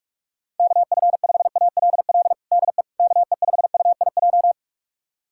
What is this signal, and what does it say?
Morse code, heard as a keyed tone, about 46 wpm, 700 Hz. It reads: KF5ALL DE KE5UIJ
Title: USA Callsign DE Callsign 45wpm